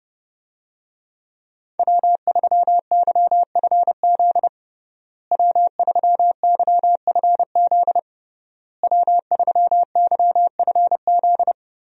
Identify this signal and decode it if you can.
Morse code, heard as a keyed tone, about 30 wpm, 700 Hz. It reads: W3YFZ W3YFZ W3YFZ
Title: USA Call Signs - 5 Characters - Repeated 3x - 30wpm